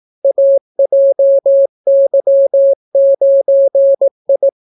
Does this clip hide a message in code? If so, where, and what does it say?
Yes: Morse code from 0.2 to 4.5 s, heard as a keyed tone, about 18 wpm, 550 Hz. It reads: AJY9I